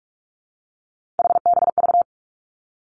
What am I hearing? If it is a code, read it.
Morse code, heard as a keyed tone, about 46 wpm, 700 Hz. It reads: HBV
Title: Three Challenging Letter-Number Combinations 45wpm